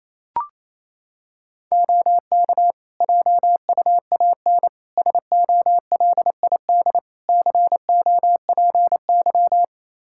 Morse code, heard as a keyed tone, about 28 wpm, 700 Hz. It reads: OK JUAN SOLID COPY
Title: Practice QSO - Exchange 3 - More Info - 28wpm